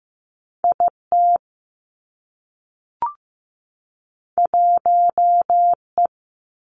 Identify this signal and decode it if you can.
Morse code, heard as a keyed tone, about 15 wpm, 700 Hz. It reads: I T 1 E
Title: Two Challenging Letter-Number Combinations 15wpm